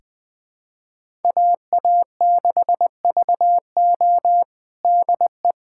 Morse code, heard as a keyed tone, about 19 wpm, 700 Hz. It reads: AA6VO DE
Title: USA Callsign DE Callsign - 5 Characters - 20wpm